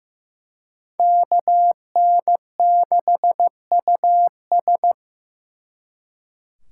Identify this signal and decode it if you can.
Morse code, heard as a keyed tone, about 18 wpm, 700 Hz. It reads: KN6US
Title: Speed Racing - USA 2x2 Call Signs 15wpm